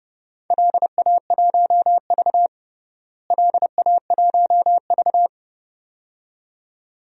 Morse code, heard as a keyed tone, about 32 wpm, 700 Hz. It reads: LA1V LA1V
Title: DXCC-100 Call Signs - 4 Characters - Repeated 2x - 30wpm